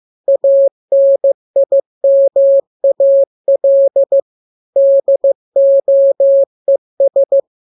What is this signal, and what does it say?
Morse code, heard as a keyed tone, about 15 wpm, 550 Hz. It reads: ANIMAL DOES
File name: jokes / joke 26 15 0000